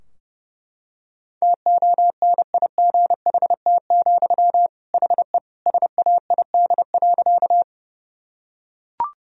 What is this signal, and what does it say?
Morse code, heard as a keyed tone, about 30 wpm, 700 Hz. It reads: TONIGHT, HE SAID.
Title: Sentences from Top 1000 Words - Review Part 3 - 30wpm